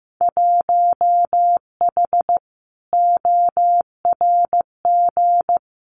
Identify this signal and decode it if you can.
Morse code, heard as a keyed tone, about 15 wpm, 700 Hz. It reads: 1H ORG
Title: Winter Field Day Exchanges 15wpm